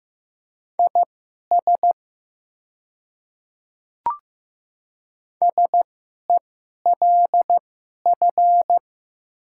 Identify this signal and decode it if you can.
Morse code, heard as a keyed tone, about 15 wpm, spaced as for 11 wpm, 700 Hz. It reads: IS SELF
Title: 2x Character Spacing - Top 200 Words - 15wpm